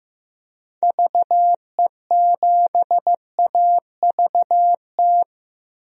Morse code, heard as a keyed tone, about 15 wpm, 700 Hz. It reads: VE7AVT
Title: DXCC-100 Call Signs - 6 Characters - Repeated 3x - 15wpm